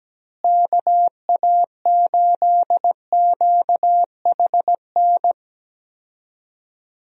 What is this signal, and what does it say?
Morse code, heard as a keyed tone, about 17 wpm, 700 Hz. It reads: KA8QHN
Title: USA Call Signs - 6 Characters - Repeated 3x - 17wpm